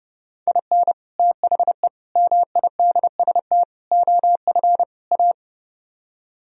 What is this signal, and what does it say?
Morse code, heard as a keyed tone, about 30 wpm, 700 Hz. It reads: IN THE MIDST OF A